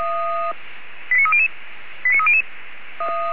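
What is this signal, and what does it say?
Five-tone selective call (ZVEI), heard as quick runs of five tones, tones of about 70 ms. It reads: 89300 89300
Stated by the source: ZVEI five-tone selective call received by SDR, recorded on the audio server of Feuerwehr Orlamünde.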